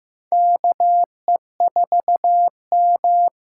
Morse code, heard as a keyed tone, about 15 wpm, 700 Hz. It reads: KE4M
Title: USA Call Signs - 6 Characters - 6x Character Spacing - 15wpm